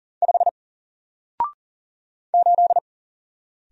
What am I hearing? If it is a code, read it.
Morse code, heard as a keyed tone, about 40 wpm, 700 Hz. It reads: 5 8
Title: Single Letter-Number Alt 40wpm